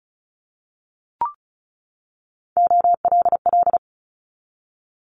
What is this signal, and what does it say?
Morse code, heard as a keyed tone, about 36 wpm, 700 Hz. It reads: OLL
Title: Multi-letter Phonemes 35wpm